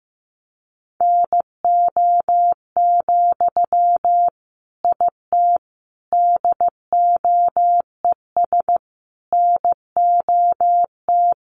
Morse code, heard as a keyed tone, about 15 wpm, 700 Hz. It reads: NO, IT DOES NOT
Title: Sentences from Top 1000 Words 15wpm